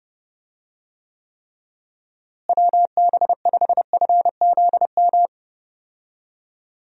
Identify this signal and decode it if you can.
Morse code, heard as a keyed tone, about 30 wpm, 700 Hz. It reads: WB5FZM
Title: USA Call Signs - 6 Characters - Repeated 3x - 30wpm